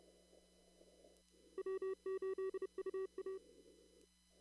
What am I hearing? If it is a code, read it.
Morse code, heard as a keyed tone, about 31 wpm, 380 Hz. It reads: W8UA
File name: CW / kt5j wpx cw-op-k5pi